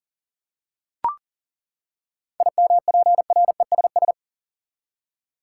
Morse code, heard as a keyed tone, about 38 wpm, 700 Hz. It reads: IMPRESS
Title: Seven Letter Words - 40wpm